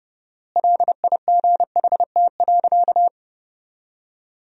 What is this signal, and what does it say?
Morse code, heard as a keyed tone, about 30 wpm, 700 Hz. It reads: LIGHT.
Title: 4x Word-Spacing - Sentences from Top 200 Words - 30wpm